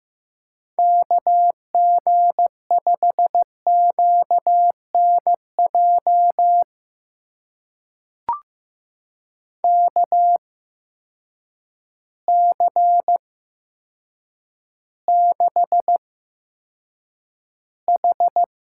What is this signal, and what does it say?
Morse code, heard as a keyed tone, about 15 wpm, spaced as for 8 wpm, 700 Hz. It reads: KG5QNJ KC6H
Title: USA Call Signs - 6 Characters - 8x Character Spacing - 15wpm